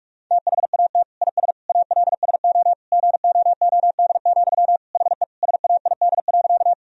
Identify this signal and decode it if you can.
Morse code, heard as a keyed tone, about 48 wpm, 700 Hz. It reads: THAT IS ALSO GOOD, HE SAID.
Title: Sentences from Top 1000 Words - Review Part 1 - 45wpm